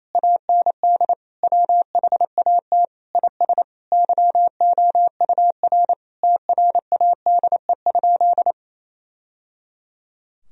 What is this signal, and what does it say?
Morse code, heard as a keyed tone, about 28 wpm, 700 Hz. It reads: AND WHAT IS YOUR TRADE?
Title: Sentences from Top 1000 Words - Review Part 4 - 28wpm